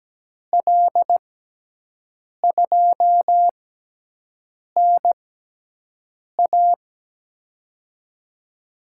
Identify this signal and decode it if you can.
Morse code, heard as a keyed tone, about 17 wpm, spaced as for 6 wpm, 700 Hz. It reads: L2NA
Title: USA Call Signs - 5 Characters - 6x Character Spacing - 17wpm